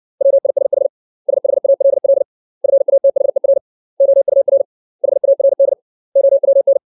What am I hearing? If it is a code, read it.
Morse code, heard as a keyed tone, about 60 wpm, 550 Hz. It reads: WISH SHALL FATHER OUR HARD OWN
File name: 632 words / 632 Words 60WPM